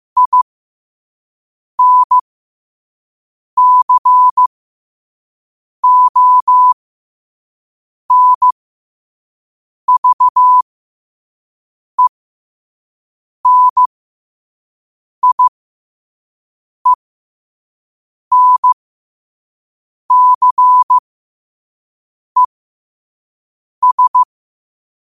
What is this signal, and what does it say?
Morse code, heard as a keyed tone, about 15 wpm, spaced as for 5 wpm, 1000 Hz. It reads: INCONVENIENCES
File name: qotdfiles / 5wpm